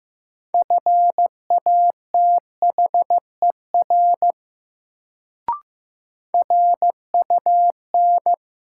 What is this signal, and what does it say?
Morse code, heard as a keyed tone, about 15 wpm, 700 Hz. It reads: FATHER RUN
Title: Sentences from Top 300 Words - New Word Drill - 15wpm